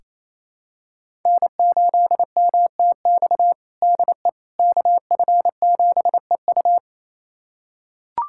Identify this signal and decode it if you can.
Morse code, heard as a keyed tone, about 28 wpm, 700 Hz. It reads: N8MTX DE KF7EU
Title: USA Callsign DE Callsign - 5 Characters - 28wpm